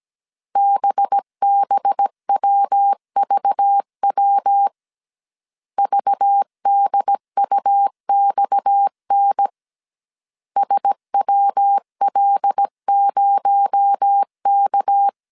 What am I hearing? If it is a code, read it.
Morse code, heard as a keyed tone, about 17 wpm, 800 Hz. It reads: BBWVW VDUXN SWL0K
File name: CW / DET / 1 6 1500C S A 16wpm 2635